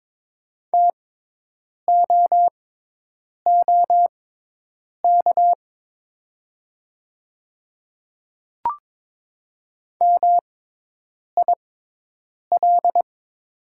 Morse code, heard as a keyed tone, about 22 wpm, spaced as for 8 wpm, 700 Hz. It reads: TOOK MIL
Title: 6x Character Spacing - Top 500 Words - 22wpm